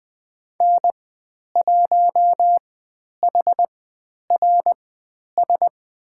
Morse code, heard as a keyed tone, about 20 wpm, spaced as for 10 wpm, 700 Hz. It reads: N1HRS
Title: 109 Lesson 18g - Letter m - Call Signs 20wpm